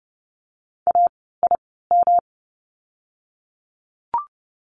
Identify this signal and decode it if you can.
Morse code, heard as a keyed tone, about 30 wpm, spaced as for 17 wpm, 700 Hz. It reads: AIM
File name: course / 30:10wpm / 108 Lesson 18f - Letter m - Words 30wpm